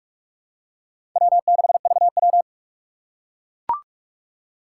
Morse code, heard as a keyed tone, about 44 wpm, 700 Hz. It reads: W6VW
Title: USA Call Signs - 4 Characters - 2x Character Spacing - 45wpm